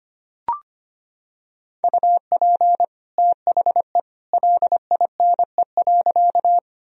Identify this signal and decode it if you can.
Morse code, heard as a keyed tone, about 25 wpm, 700 Hz. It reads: UP THE LINE.
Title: Sentences from Top 1000 Words - Review Part 1 - 25wpm